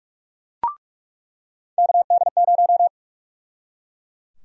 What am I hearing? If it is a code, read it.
Morse code, heard as a keyed tone, about 44 wpm, 700 Hz. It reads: KD0
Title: USA Call Sign Prefixes - 45wpm